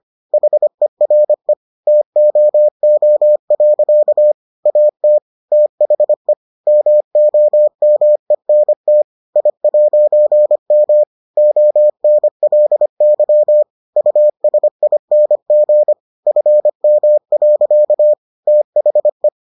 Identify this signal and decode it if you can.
Morse code, heard as a keyed tone, about 25 wpm, 600 Hz. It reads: HERE TOO. AT THE MOMENT I'M ONLY USING FM. THE